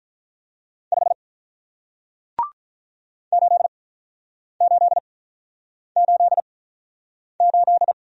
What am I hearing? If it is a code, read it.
Morse code, heard as a keyed tone, about 46 wpm, spaced as for 9 wpm, 700 Hz. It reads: 5 8888